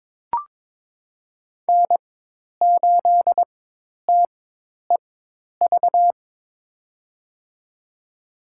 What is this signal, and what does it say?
Morse code, heard as a keyed tone, about 22 wpm, spaced as for 10 wpm, 700 Hz. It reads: N8TEV
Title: USA Call Signs - 4x Character Spacing - 22wpm